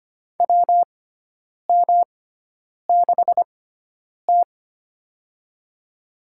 Morse code, heard as a keyed tone, about 25 wpm, spaced as for 9 wpm, 700 Hz. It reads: WM6T
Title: USA Call Signs - 6x Character Spacing - 25wpm